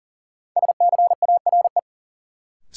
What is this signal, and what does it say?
Morse code, heard as a keyed tone, about 40 wpm, 700 Hz. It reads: SCARE
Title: Sentences from Top 1000 Words - Review Part 5 - New Word Drill - 40wpm